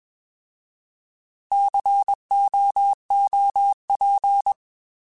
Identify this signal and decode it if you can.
Morse code, heard as a keyed tone, about 21 wpm, 800 Hz. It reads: COOP